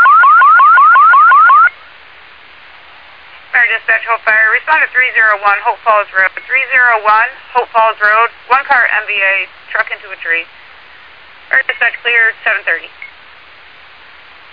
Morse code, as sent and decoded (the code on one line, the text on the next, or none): none